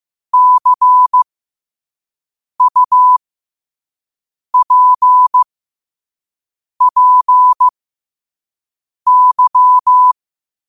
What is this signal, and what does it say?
Morse code, heard as a keyed tone, about 15 wpm, spaced as for 5 wpm, 1000 Hz. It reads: CUPPY
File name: qotdfiles / 5wpm